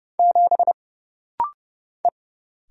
Morse code, heard as a keyed tone, about 30 wpm, 700 Hz. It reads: 7 E
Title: Single Letter-Number - Rapid-Fire 30wpm